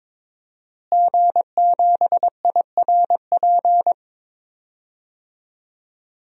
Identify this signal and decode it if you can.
Morse code, heard as a keyed tone, about 21 wpm, 700 Hz. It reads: G7IRP
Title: DXCC-100 Call Signs - 5 Characters - Repeated 3x - 22wpm